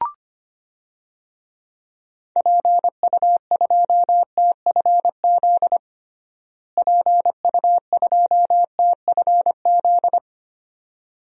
Morse code, heard as a keyed tone, about 25 wpm, 700 Hz. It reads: PU2TFZ PU2TFZ
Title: DXCC-100 Call Signs - 6 Characters - Repeated 2x - 25wpm